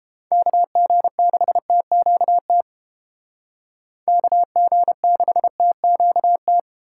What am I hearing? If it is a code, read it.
Morse code, heard as a keyed tone, about 30 wpm, 700 Hz. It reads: KG6TQT KG6TQT